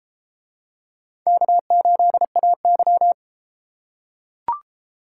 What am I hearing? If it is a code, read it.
Morse code, heard as a keyed tone, about 32 wpm, 700 Hz. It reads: K8AY